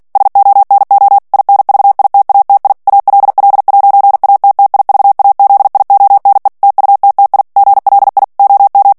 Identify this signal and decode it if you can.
Morse code, heard as a keyed tone, about 48 wpm, 800 Hz. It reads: SONO INVITATI ALL'ATTIVAZIONE TUTTI GLI OM
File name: morse mp3 / ITALIEN texte clair 1 vitesse 45 mots minute